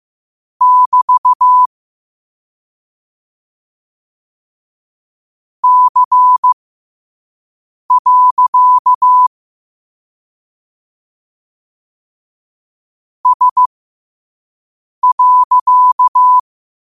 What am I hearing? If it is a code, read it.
Morse code, heard as a keyed tone, about 15 wpm, spaced as for 5 wpm, 1000 Hz. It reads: <BT> C. S.